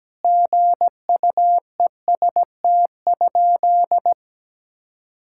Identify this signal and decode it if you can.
Morse code, heard as a keyed tone, about 17 wpm, 700 Hz. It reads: GUEST?
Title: Sentences from Top 1600 Words 17wpm